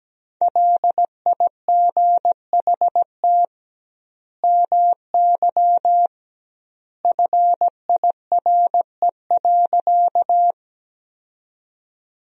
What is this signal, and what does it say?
Morse code, heard as a keyed tone, about 17 wpm, 700 Hz. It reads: LIGHT MY FIRE.
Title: 2x Word-Spacing - Sentences from Top 400 Words - 17wpm